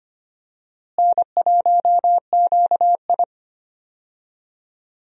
Morse code, heard as a keyed tone, about 25 wpm, 700 Hz. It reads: N1QI
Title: USA Call Signs - 4 Characters - Repeated 3x - 25wpm